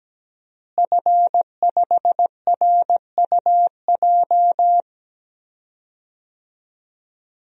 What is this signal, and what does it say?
Morse code, heard as a keyed tone, about 17 wpm, 700 Hz. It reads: F5RUJ
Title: DXCC-100 Call Signs - 5 Characters - Repeated 3x - 17wpm